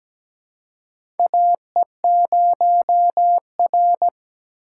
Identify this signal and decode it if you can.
Morse code, heard as a keyed tone, about 17 wpm, 700 Hz. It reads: AE0R
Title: USA Call Signs - 4 Characters - 4x Character Spacing - 17wpm